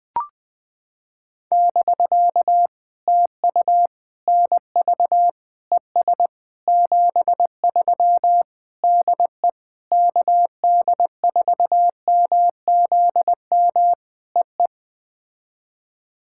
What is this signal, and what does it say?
Morse code, heard as a keyed tone, about 18 wpm, 700 Hz. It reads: <BK> TU NV ES 73 DE KD4MZM EE
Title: POTA QSO Practice - 20wpm